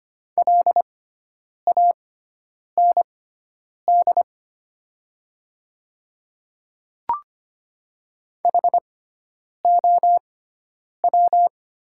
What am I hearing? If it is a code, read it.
Morse code, heard as a keyed tone, about 25 wpm, spaced as for 9 wpm, 700 Hz. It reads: LAND HOW
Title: 6x Character Spacing - Top 200 Words - 25wpm